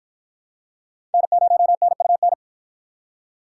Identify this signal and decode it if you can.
Morse code, heard as a keyed tone, about 53 wpm, 700 Hz. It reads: N0NUN